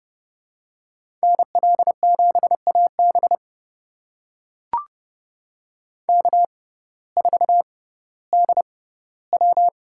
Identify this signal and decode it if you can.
Morse code, heard as a keyed tone, about 30 wpm, spaced as for 16 wpm, 700 Hz. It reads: NL7AB K4DW
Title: USA Call Signs - 6x Character Spacing - 30wpm